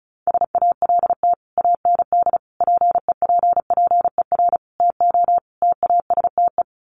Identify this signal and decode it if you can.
Morse code, heard as a keyed tone, about 34 wpm, 700 Hz. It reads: SALT AND PEPPER TO TASTE
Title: Sets of 5 Words 35wpm